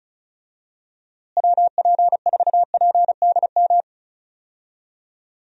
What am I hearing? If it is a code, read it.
Morse code, heard as a keyed tone, about 36 wpm, 700 Hz. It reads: WP4PDM
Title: USA Call Signs - 6 Characters - Repeated 3x - 35wpm